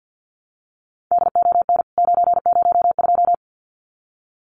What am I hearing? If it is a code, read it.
Morse code, heard as a keyed tone, about 50 wpm, 700 Hz. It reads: DON 902
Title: CWT Exchange 50wpm